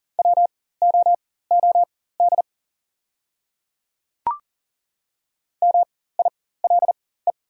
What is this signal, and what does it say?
Morse code, heard as a keyed tone, about 39 wpm, spaced as for 18 wpm, 700 Hz. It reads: WOOD MILE
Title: 4x Character Spacing - Top 300 Words - 40wpm